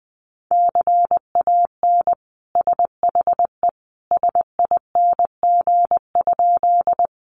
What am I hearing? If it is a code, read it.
Morse code, heard as a keyed tone, about 20 wpm, 700 Hz. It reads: CAN SHE SING?
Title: Sentences from Top 400 Words 20wpm